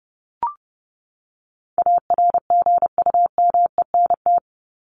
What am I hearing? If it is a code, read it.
Morse code, heard as a keyed tone, about 30 wpm, 700 Hz. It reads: ARGUMENT
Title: Top 1251-1500 Words 30wpm